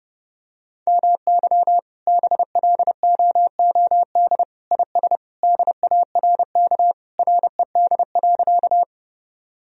Morse code, heard as a keyed tone, about 30 wpm, 700 Hz. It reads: MY BLOOD IS DARK RED.
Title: Sentences from Top 1000 Words - Review Part 2 - 30wpm